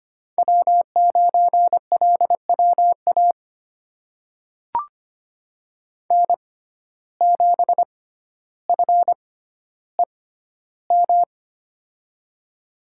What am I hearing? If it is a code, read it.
Morse code, heard as a keyed tone, about 25 wpm, spaced as for 13 wpm, 700 Hz. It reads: W9LWA N7FEM
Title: USA Call Signs - 5 Characters - 6x Character Spacing - 25wpm